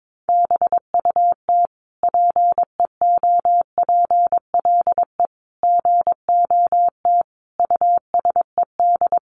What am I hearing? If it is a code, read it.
Morse code, heard as a keyed tone, about 22 wpm, 700 Hz. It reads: BUT PEOPLE GOT USED
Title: Sentences from Top 700 Words 22wpm